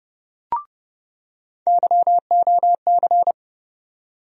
Morse code, heard as a keyed tone, about 31 wpm, 700 Hz. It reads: YOC
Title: USA Call Sign Postfixes - 3 Letters 30wpm